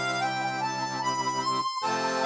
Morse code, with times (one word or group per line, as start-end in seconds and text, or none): none